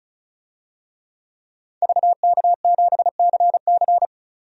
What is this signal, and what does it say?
Morse code, heard as a keyed tone, about 36 wpm, 700 Hz. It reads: VK7CC
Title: DXCC-100 Call Signs - 5 Characters - 35wpm